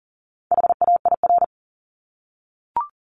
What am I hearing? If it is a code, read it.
Morse code, heard as a keyed tone, about 38 wpm, 700 Hz. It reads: HAIR
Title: Top 1751-2000 Words 40wpm